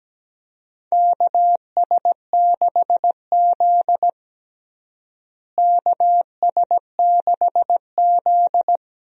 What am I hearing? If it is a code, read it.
Morse code, heard as a keyed tone, about 17 wpm, 700 Hz. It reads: KS6Z KS6Z
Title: USA Call Signs - 4 Characters - Repeated 2x - 17wpm